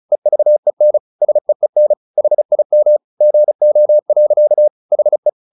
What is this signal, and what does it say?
Morse code, heard as a keyed tone, about 34 wpm, 600 Hz. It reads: EVEN SEEN HIM GO. HE
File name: the old man and the sea / the old man and the sea 1 35wpm